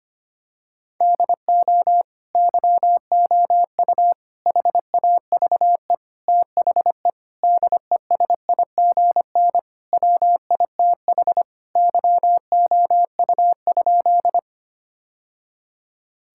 Morse code, heard as a keyed tone, about 25 wpm, 700 Hz. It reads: DO YOU HAVE THE DESIGN WITH YOU?
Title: Sentences from Top 1000 Words - Review Part 5 - 25wpm